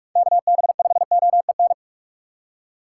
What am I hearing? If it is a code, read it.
Morse code, heard as a keyed tone, about 44 wpm, 700 Hz. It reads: KB5OEN
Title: USA Call Signs - 6x Character Spacing - 45wpm